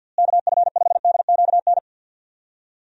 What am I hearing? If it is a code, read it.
Morse code, heard as a keyed tone, about 47 wpm, 700 Hz. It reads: KF5DQN
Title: USA Call Signs - 6 Characters - Repeated 2x - 50wpm